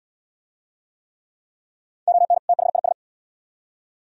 Morse code, heard as a keyed tone, about 68 wpm, 700 Hz. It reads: ON THIS